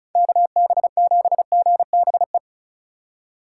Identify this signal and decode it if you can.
Morse code, heard as a keyed tone, about 36 wpm, 700 Hz. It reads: KB7GBE
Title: USA Call Signs - 6 Characters - 4x Character Spacing - 35wpm